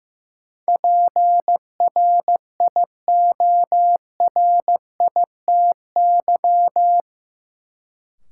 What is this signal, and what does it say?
Morse code, heard as a keyed tone, about 15 wpm, 700 Hz. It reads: PRIORITY
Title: Sentences from Top 1600 Words - New Word Drill Alt - 15wpm